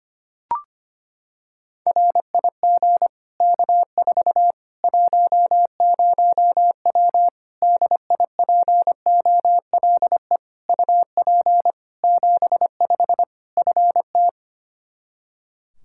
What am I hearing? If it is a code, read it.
Morse code, heard as a keyed tone, about 25 wpm, 700 Hz. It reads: RIG K4 10W DIPOLE UP 75 FT
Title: Practice QSO - Exchange 2 - Info - 25wpm